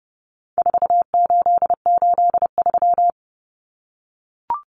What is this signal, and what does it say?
Morse code, heard as a keyed tone, about 30 wpm, 700 Hz. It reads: 4883
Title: Numbers - 4-Digit 30wpm